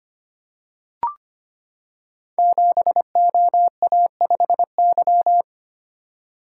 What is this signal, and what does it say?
Morse code, heard as a keyed tone, about 25 wpm, 700 Hz. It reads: 7OA5Y
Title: Code Groups - 5 Letter:Numbers - 25wpm